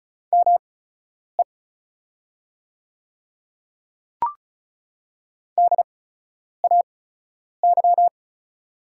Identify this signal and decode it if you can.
Morse code, heard as a keyed tone, about 34 wpm, spaced as for 10 wpm, 690 Hz. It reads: ME DAY
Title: 8x Character Spacing - Top 200 Words - 35wpm